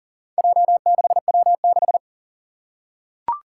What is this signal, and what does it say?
Morse code, heard as a keyed tone, about 40 wpm, 690 Hz. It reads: J6W6
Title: Four Challenging Letter-Number Combinations 40wpm